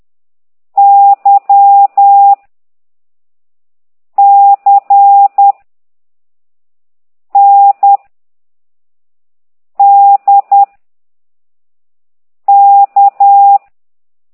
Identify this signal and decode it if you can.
Morse code, heard as a keyed tone, about 10 wpm, 800 Hz. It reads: Y C N D K